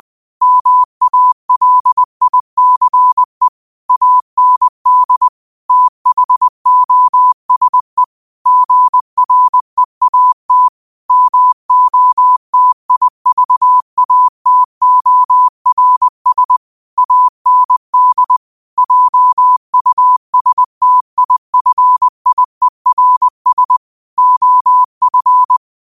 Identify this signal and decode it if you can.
Morse code, heard as a keyed tone, about 20 wpm, 1000 Hz. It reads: MALICE AND THOSE GREAT MOTIVATORS AND JUSTIFIERS OF
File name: qotdfiles / 20wpm